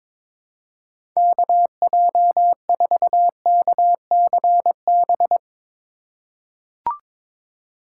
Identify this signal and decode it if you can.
Morse code, heard as a keyed tone, about 22 wpm, 700 Hz. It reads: KJ4KCB
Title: USA Call Signs - 6 Characters - 3x Character Spacing - 22wpm